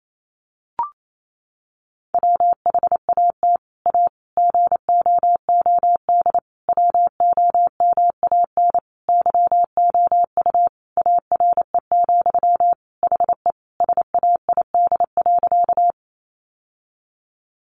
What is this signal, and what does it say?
Morse code, heard as a keyed tone, about 28 wpm, 700 Hz. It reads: WHAT A GOOD WOMAN YOU ARE, HE SAID.
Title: Sentences from Top 1100 Words 28wpm